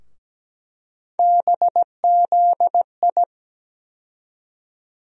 Morse code, heard as a keyed tone, about 17 wpm, 700 Hz. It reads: BZI
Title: USA Call Sign Postfixes - 3 Letters 17wpm